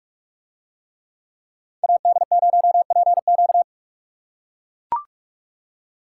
Morse code, heard as a keyed tone, about 44 wpm, 700 Hz. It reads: AD0PQ